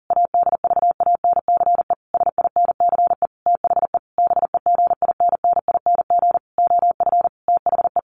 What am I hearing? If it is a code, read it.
Morse code, heard as a keyed tone, about 40 wpm, 700 Hz. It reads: ADVANCE SINCE THE BEGINNING OF THE
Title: Sentences from Top 1400 Words 40wpm